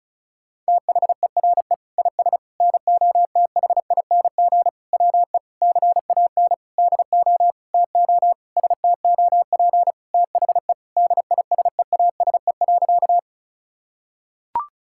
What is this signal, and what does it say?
Morse code, heard as a keyed tone, about 34 wpm, 700 Hz. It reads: THERE IS NOTHING WE CAN DO TO STOP THE DISEASE.
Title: Sentences from Top 1000 Words - Review Part 6 - 35wpm